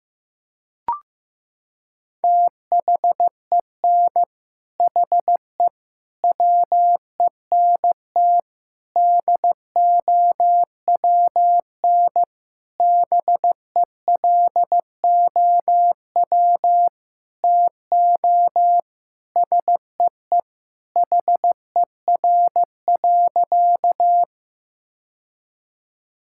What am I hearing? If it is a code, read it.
Morse code, heard as a keyed tone, about 15 wpm, 700 Hz. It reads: THEN HE WENT DOWN BELOW TO SEE HER.
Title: Sentences from Top 1000 Words - Review Part 5 - 15wpm